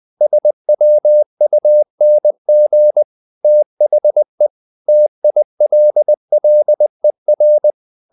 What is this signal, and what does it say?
Morse code, heard as a keyed tone, about 20 wpm, 600 Hz. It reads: SWUNG THE TILLER
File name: the old man and the sea part 2 20wpm